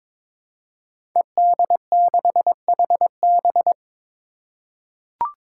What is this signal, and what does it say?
Morse code, heard as a keyed tone, about 22 wpm, 700 Hz. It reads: ED6HB
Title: Five Challenging Letter-Number Combinations 22wpm